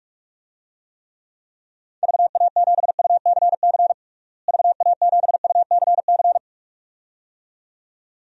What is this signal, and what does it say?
Morse code, heard as a keyed tone, about 47 wpm, 700 Hz. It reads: VA7UCC VA7UCC
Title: DXCC-100 Call Signs - 6 Characters - Repeated 2x - 45wpm